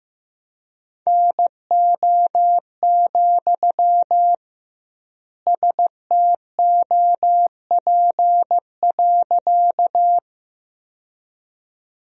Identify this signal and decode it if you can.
Morse code, heard as a keyed tone, about 16 wpm, 700 Hz. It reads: NO, STOP.
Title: 2x Word-Spacing - Sentences from Top 300 Words - 15wpm